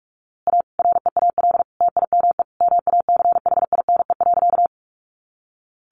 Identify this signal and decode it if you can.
Morse code, heard as a keyed tone, about 44 wpm, 700 Hz. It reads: A REAL TIME MACHINE.
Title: A Sound Of Thunder 45wpm